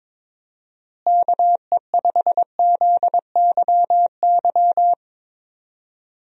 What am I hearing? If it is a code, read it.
Morse code, heard as a keyed tone, about 22 wpm, 700 Hz. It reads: KE5ZYY
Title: USA Call Signs - 6 Characters - Repeated 2x - 22wpm